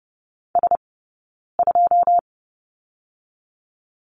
Morse code, heard as a keyed tone, about 30 wpm, 700 Hz. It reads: S 2
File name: course / 30:10wpm / 086 Lesson 15d - Number 5 - Two Characters 30wpm